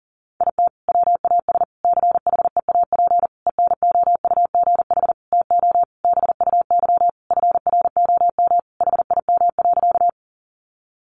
Morse code, heard as a keyed tone, about 40 wpm, 700 Hz. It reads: IT WAS CHEAP ENOUGH TO BUY FROM HIM.
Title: Sentences from Top 1200 Words 40wpm